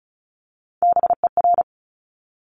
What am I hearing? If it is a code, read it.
Morse code, heard as a keyed tone, about 36 wpm, 700 Hz. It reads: BER
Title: Top Three Letter Combinations in Words 35wpm